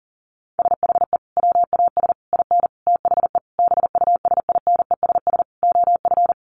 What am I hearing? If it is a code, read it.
Morse code, heard as a keyed tone, about 40 wpm, 700 Hz. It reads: SHE WAS IN THE BUSINESS OF
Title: Sentences from Top 1500 Words 40wpm